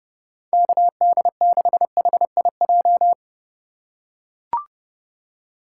Morse code, heard as a keyed tone, about 32 wpm, 700 Hz. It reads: KD6HIJ
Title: USA Call Signs - 3x Character Spacing - 30wpm